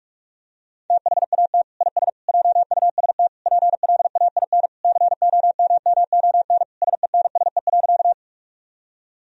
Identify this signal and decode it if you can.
Morse code, heard as a keyed tone, about 47 wpm, 700 Hz. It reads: THAT IS JUST PLAIN COMMON SENSE.
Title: Sentences from Top 1000 Words - Review Part 4 - 45wpm